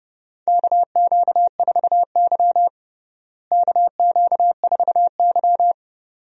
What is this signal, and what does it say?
Morse code, heard as a keyed tone, about 30 wpm, 700 Hz. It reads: KQ4Y KQ4Y
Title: USA Call Signs - 4 Characters - Repeated 2x - 30wpm